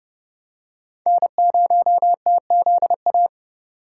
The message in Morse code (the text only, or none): N0TZA